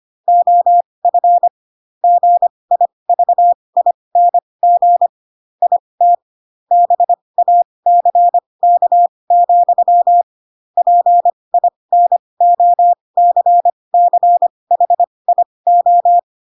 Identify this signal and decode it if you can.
Morse code, heard as a keyed tone, about 25 wpm, spaced as for 20 wpm, 700 Hz. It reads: OF GIVING IT BACK, PINOCCHIO
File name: Pinocchio / Pinocchio ch3 15 25 ik1hge